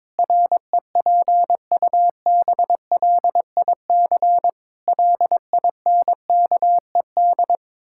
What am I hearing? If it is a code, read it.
Morse code, heard as a keyed tone, about 22 wpm, 700 Hz. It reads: REPUBLIC LINKED
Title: Pangram Challenge v2 - 22wpm